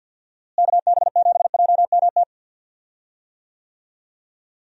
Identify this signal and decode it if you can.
Morse code, heard as a keyed tone, about 52 wpm, 700 Hz. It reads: KB7JMT
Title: USA Call Signs - 6 Characters - Repeated 3x - 50wpm